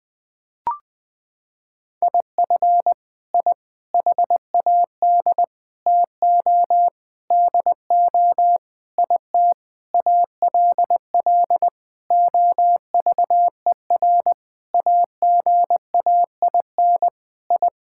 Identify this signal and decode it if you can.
Morse code, heard as a keyed tone, about 20 wpm, 700 Hz. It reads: IF I HAD TO DO IT ALL OVER AGAIN I
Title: Sentences with 9-Word Phrases - 20wpm